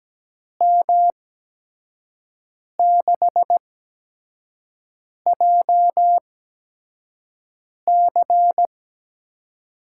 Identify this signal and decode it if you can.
Morse code, heard as a keyed tone, about 17 wpm, spaced as for 5 wpm, 700 Hz. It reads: M6JC
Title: USA Call Signs - 6 Characters - 8x Character Spacing - 17wpm